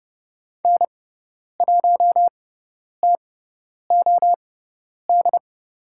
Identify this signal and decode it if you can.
Morse code, heard as a keyed tone, about 30 wpm, spaced as for 10 wpm, 700 Hz. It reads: N1TOD
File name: Lesson 16g - Letter c - Call Signs 30wpm